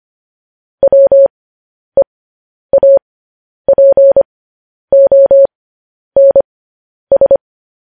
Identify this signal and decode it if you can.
Morse code, heard as a keyed tone, about 25 wpm, spaced as for 10 wpm, 550 Hz. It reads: WEAPONS
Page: news headlines in morse code at 10 wpm